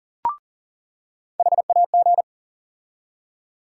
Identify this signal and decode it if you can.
Morse code, heard as a keyed tone, about 40 wpm, 700 Hz. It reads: HAG
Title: USA Call Sign Postfixes - 3 Letters 40wpm